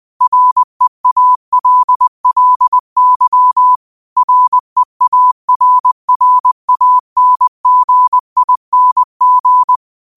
Morse code, heard as a keyed tone, about 20 wpm, 1000 Hz. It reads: REALLY REARRANGING